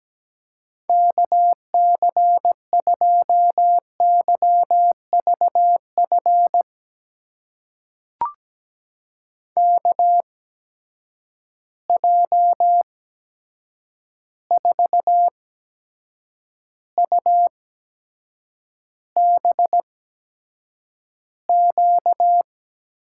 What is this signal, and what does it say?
Morse code, heard as a keyed tone, about 17 wpm, spaced as for 7 wpm, 700 Hz. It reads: KC2YVF KJ4UBQ
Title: USA Call Signs - 6 Characters - 8x Character Spacing - 17wpm